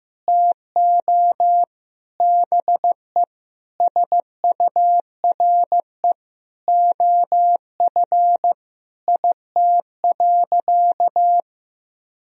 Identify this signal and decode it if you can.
Morse code, heard as a keyed tone, about 15 wpm, 700 Hz. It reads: TO BE SURE OF IT.
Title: Sentences from Top 1000 Words - Review Part 4 - 15wpm